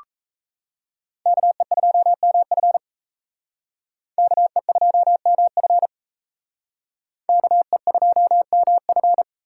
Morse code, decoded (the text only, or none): KE2MF KE2MF KE2MF